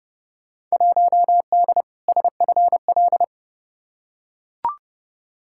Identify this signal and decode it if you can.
Morse code, heard as a keyed tone, about 32 wpm, 700 Hz. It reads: 1D SFL
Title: Field Day Exchanges - 30wpm